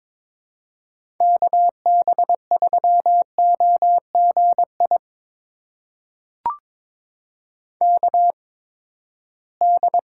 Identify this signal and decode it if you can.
Morse code, heard as a keyed tone, about 22 wpm, spaced as for 15 wpm, 700 Hz. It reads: KB3OGI KD
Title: USA Call Signs - 6 Characters - 8x Character Spacing - 22wpm